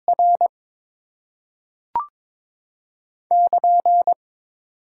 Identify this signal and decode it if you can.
Morse code, heard as a keyed tone, about 22 wpm, 700 Hz. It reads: R <KN>
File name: Top 90 QSO Elements - 6x Character Spacing - 22wpm